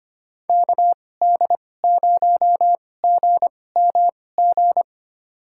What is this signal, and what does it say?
Morse code, heard as a keyed tone, about 25 wpm, spaced as for 18 wpm, 700 Hz. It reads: KD0GMG
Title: USA Call Signs - 6 Characters - 2x Character Spacing - 25wpm